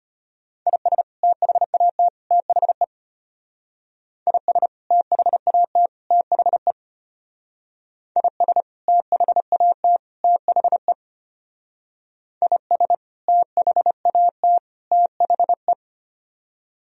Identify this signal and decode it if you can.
Morse code, heard as a keyed tone, about 32 wpm, 700 Hz. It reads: IS THAT THE IS THAT THE IS THAT THE IS THAT THE